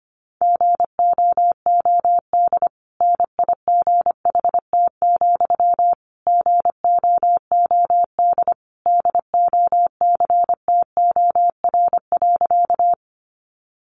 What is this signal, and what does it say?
Morse code, heard as a keyed tone, about 25 wpm, 700 Hz. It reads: GOOD NIGHT, GOOD DOCTOR.
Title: Sentences from Top 1500 Words 25wpm